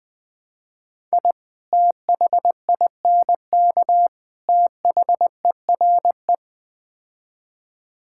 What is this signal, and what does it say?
Morse code, heard as a keyed tone, about 21 wpm, 700 Hz. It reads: I THINK THERE
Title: Build A Phrase - 5-Word Phrases - 20wpm